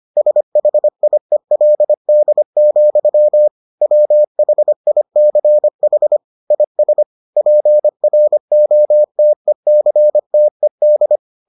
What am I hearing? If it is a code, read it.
Morse code, heard as a keyed tone, about 25 wpm, 600 Hz. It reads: SHIELD, WHICH IS PROTECTED